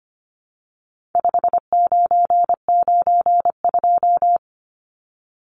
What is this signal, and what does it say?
Morse code, heard as a keyed tone, about 25 wpm, 700 Hz. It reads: 5992
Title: Numbers - 4-Digit 25wpm